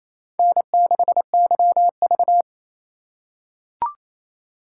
Morse code, heard as a keyed tone, about 28 wpm, 700 Hz. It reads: N6YV
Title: USA Call Signs - 4 Characters - 4x Character Spacing - 28wpm